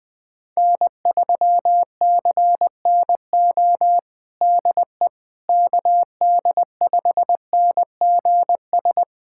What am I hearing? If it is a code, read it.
Morse code, heard as a keyed tone, about 20 wpm, 700 Hz. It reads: N3CNO DE KD5NGS
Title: USA Callsign DE Callsign 20wpm